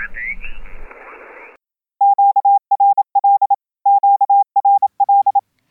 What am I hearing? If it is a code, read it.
Morse code, heard as a keyed tone, about 27 wpm, 880 Hz. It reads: QRL QRL